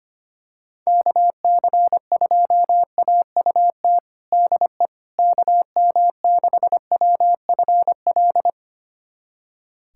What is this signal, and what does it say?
Morse code, heard as a keyed tone, about 25 wpm, 700 Hz. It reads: KC2AUT DE KM6WFL
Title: USA Callsign DE Callsign - 6 Characters - 25wpm